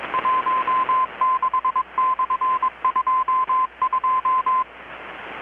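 Morse code, heard as a keyed tone, about 21 wpm, 1030 Hz, with radio noise. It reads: 16/22
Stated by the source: Morse from KPH's Night of Nights XX maritime radiotelegraphy event, received in Chiapas, Mexico about 3,800 km away on a Kenwood R-600 with a grounded 10 m long-wire antenna.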